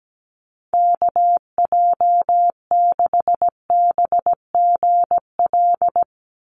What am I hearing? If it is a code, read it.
Morse code, heard as a keyed tone, about 17 wpm, 700 Hz. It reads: KJ6BGL
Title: USA Call Signs 17wpm